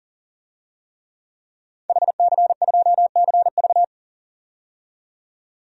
Morse code, heard as a keyed tone, about 40 wpm, 700 Hz. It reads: HC2CV
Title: DXCC-100 Call Signs - 5 Characters - 40wpm